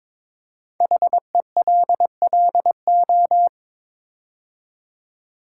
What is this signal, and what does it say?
Morse code, heard as a keyed tone, about 22 wpm, 700 Hz. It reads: HELLO
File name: 5 Letter Words - First 3 Letters Given - 22wpm